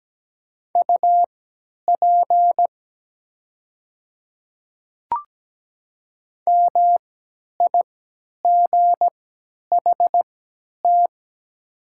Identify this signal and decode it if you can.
Morse code, heard as a keyed tone, about 17 wpm, spaced as for 10 wpm, 700 Hz. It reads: UP MIGHT
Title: 3x Character Spacing - Top 300 Words - 17wpm